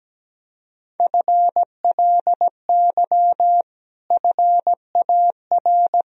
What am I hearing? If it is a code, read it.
Morse code, heard as a keyed tone, about 17 wpm, 700 Hz. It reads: FLY FAR
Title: Sentences from Top 1000 Words - Review Part 2 - 17wpm